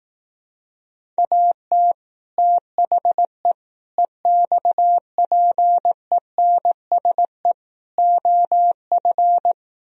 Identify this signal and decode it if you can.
Morse code, heard as a keyed tone, about 18 wpm, 700 Hz. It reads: AT THE EXPENSE OF
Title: Speed-Racing - Sets of 4 Words 15wpm